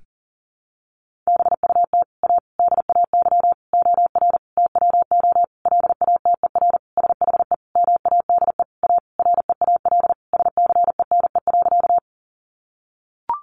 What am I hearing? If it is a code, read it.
Morse code, heard as a keyed tone, about 40 wpm, 700 Hz. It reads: BUT A DAY OR TWO LATER SHE MADE A REAL SCENE.
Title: Sentences from Top 1100 Words 40wpm